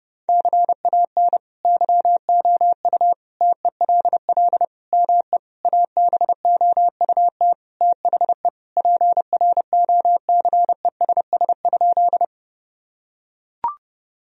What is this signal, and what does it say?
Morse code, heard as a keyed tone, about 30 wpm, 700 Hz. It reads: CAN YOU TELL ME ABOUT THE PROCESS?
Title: Sentences from Top 1000 Words - Review Part 3 - 30wpm